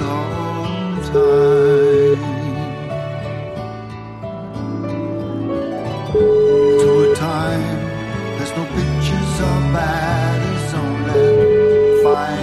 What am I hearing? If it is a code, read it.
Ringback tone (Europe, 425 Hz), with noise.